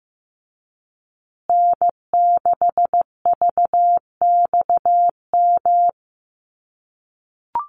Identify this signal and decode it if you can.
Morse code, heard as a keyed tone, about 15 wpm, 700 Hz. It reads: N6VXM
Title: USA Call Signs Repeated 3x 15wpm